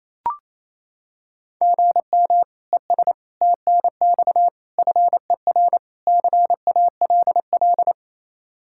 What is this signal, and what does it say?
Morse code, heard as a keyed tone, about 28 wpm, 700 Hz. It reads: GM ES TNX FER CALL
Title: Practice QSO - Exchange 1 - Beginning - 28wpm